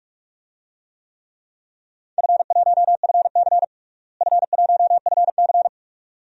Morse code, heard as a keyed tone, about 46 wpm, 700 Hz. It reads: F1FC F1FC
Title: DXCC-100 Call Signs - 4 Characters - Repeated 2x - 45wpm